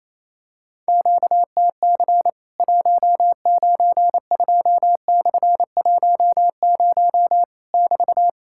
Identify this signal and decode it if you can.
Morse code, heard as a keyed tone, about 28 wpm, 700 Hz. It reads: QTC 192/10 <BT>
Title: Worked All Europe DX Contest - QTC Exchange - 28wpm